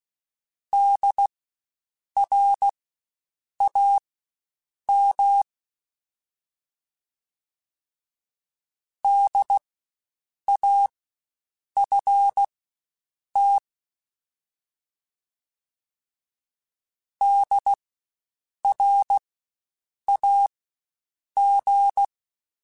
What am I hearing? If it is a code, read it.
Morse code, heard as a keyed tone, about 16 wpm, spaced as for 7 wpm, 800 Hz. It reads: DRAM DAFT DRAG
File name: Spaced Alphas / Spaced Alphas mp3s / Ds sp @15